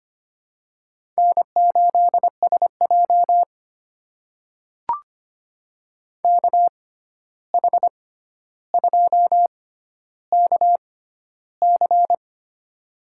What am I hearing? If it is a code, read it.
Morse code, heard as a keyed tone, about 25 wpm, spaced as for 12 wpm, 700 Hz. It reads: N8SJ KH2KC
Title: USA Call Signs - 6x Character Spacing - 25wpm